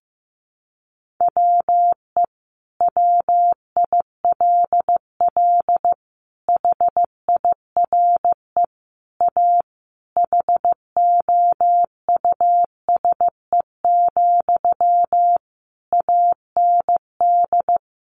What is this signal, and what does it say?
Morse code, heard as a keyed tone, about 15 wpm, 700 Hz. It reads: WE WILL HIRE A HOUSE, AND
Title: Sentences from Top 1400 Words 15wpm